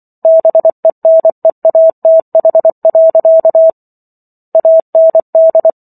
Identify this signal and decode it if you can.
Morse code, heard as a keyed tone, about 24 wpm, 650 Hz. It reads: BENEATH. AND